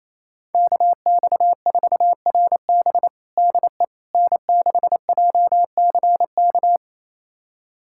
Morse code, heard as a keyed tone, about 27 wpm, 700 Hz. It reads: KX4RB DE N6JCK
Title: USA Callsign DE Callsign - 5 Characters - 28wpm